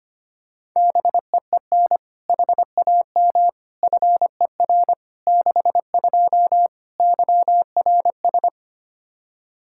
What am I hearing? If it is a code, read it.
Morse code, heard as a keyed tone, about 25 wpm, 700 Hz. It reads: BEEN HAM FER 62 YRS
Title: K1USN SST QSO Practice 25wpm